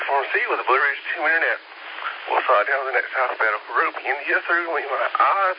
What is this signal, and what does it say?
Emergency Alert System (SAME) tones, heard as harsh buzzing bursts, with speech. It is heard from 1.2 to 4.6 s.
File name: SimplexLogic